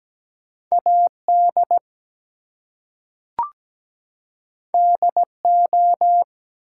Morse code, heard as a keyed tone, about 17 wpm, 700 Hz. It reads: AD DO
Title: Two Letter Words - 17wpm